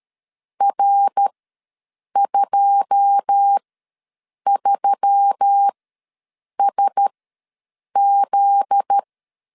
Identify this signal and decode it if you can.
Morse code, heard as a keyed tone, about 13 wpm, spaced as for 7 wpm, 800 Hz. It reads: R23SZ
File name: LEZIONI / 9 2 ws 12wpm 3406